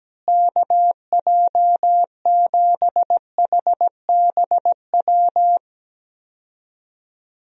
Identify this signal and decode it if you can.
Morse code, heard as a keyed tone, about 17 wpm, 700 Hz. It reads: KJ7HBW
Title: USA Call Signs - 6 Characters - Repeated 3x - 17wpm